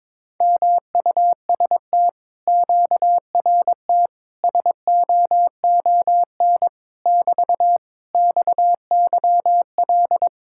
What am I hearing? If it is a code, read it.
Morse code, heard as a keyed tone, about 21 wpm, 700 Hz. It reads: MUST QRT SOON <BT> XYL
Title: Speed-Racing Common QSO Phrases Encore - 22wpm